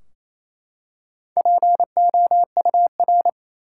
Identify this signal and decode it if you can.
Morse code, heard as a keyed tone, about 28 wpm, 700 Hz. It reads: POUR
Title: Four Letter Words - 28wpm